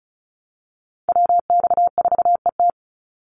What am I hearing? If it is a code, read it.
Morse code, heard as a keyed tone, about 34 wpm, 700 Hz. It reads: WX4ET
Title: NA SOTA Chaser Callsigns 35wpm